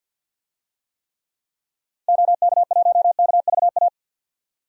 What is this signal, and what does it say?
Morse code, heard as a keyed tone, about 52 wpm, 700 Hz. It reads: OK1KVA
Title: DXCC-100 Call Signs - 6 Characters - 50wpm